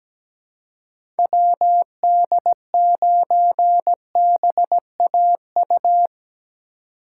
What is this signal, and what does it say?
Morse code, heard as a keyed tone, about 17 wpm, 700 Hz. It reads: WD9BAU
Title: USA Call Signs - 6 Characters - 4x Character Spacing - 17wpm